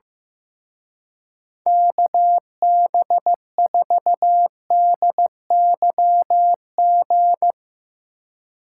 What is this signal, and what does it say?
Morse code, heard as a keyed tone, about 15 wpm, 700 Hz. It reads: KB4DYG
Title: Speed Racing - USA 2x3 Call Signs 15wpm